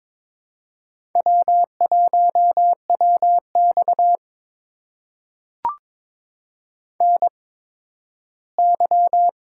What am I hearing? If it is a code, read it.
Morse code, heard as a keyed tone, about 22 wpm, spaced as for 13 wpm, 700 Hz. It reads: W1WX NY
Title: USA Call Signs - 4 Characters - 8x Character Spacing - 22wpm